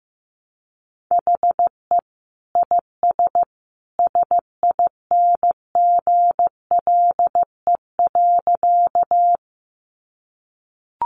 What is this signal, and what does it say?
Morse code, heard as a keyed tone, about 15 wpm, 700 Hz. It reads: HE IS SINGLE.
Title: Sentences from Top 700 Words 15wpm